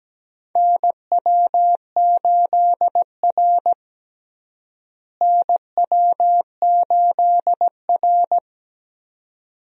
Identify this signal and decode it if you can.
Morse code, heard as a keyed tone, about 17 wpm, 700 Hz. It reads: NW8R NW8R
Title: USA Call Signs - 4 Characters - Repeated 2x - 17wpm